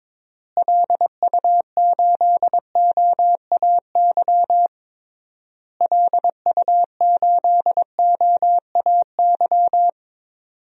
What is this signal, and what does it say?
Morse code, heard as a keyed tone, about 22 wpm, 700 Hz. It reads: LU8OAY LU8OAY
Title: DXCC-100 Call Signs - 6 Characters - Repeated 3x - 22wpm